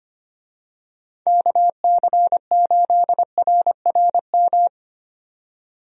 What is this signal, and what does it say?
Morse code, heard as a keyed tone, about 25 wpm, 700 Hz. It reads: KC8RRM
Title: USA Call Signs - 6 Characters - 25wpm